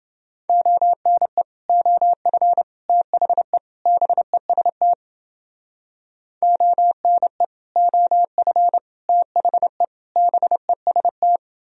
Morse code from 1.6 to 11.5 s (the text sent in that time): OF THE BEST ONE OF THE BEST